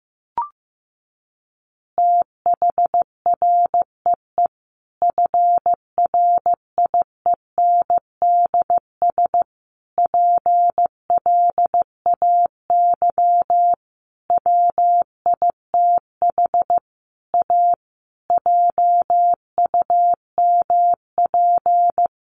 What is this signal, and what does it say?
Morse code, heard as a keyed tone, about 15 wpm, 700 Hz. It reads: THREE FRIENDS PLAY WITH A JUMP
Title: Easy Phrases 15wpm